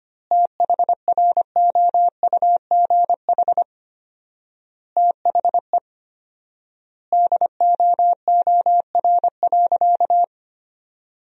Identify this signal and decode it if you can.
Morse code, heard as a keyed tone, about 25 wpm, 700 Hz. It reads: THROUGH THE DOOR.
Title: 4x Word-Spacing - Sentences from Top 400 Words - 25wpm